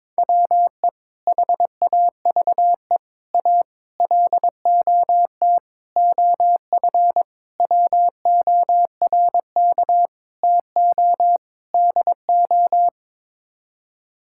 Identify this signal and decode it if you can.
Morse code, heard as a keyed tone, about 22 wpm, 700 Hz. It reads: WE HAVE A LOT OF WORK TO DO
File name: practice / 22wpm / Sets of 8 Words - 22wpm